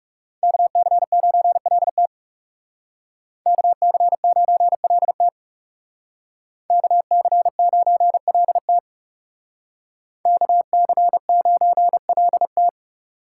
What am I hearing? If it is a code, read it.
Morse code, heard as a keyed tone, about 39 wpm, 700 Hz. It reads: KC9LT KC9LT KC9LT KC9LT